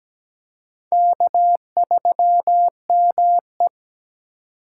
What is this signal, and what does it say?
Morse code, heard as a keyed tone, about 20 wpm, 700 Hz. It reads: K3ME
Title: Speed Racing - USA Call Signs 17wpm